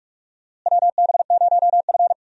Morse code, heard as a keyed tone, about 43 wpm, 700 Hz. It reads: WB0F
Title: USA Call Signs - 4 Characters - Repeated 2x - 45wpm